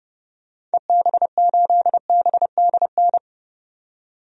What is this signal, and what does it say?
Morse code, heard as a keyed tone, about 30 wpm, 700 Hz. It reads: EB8BDN
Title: DXCC-100 Call Signs - 6 Characters - 30wpm